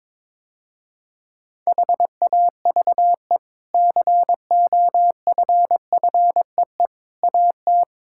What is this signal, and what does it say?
Morse code, heard as a keyed tone, about 22 wpm, 700 Hz. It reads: HAVE COFFEE AT
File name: practice / 22wpm / Easy Sentences with Hints - 22wpm